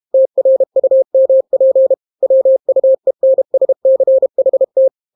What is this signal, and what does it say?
Morse code, heard as a keyed tone, about 32 wpm, 530 Hz. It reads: TRUMP WUENSCHT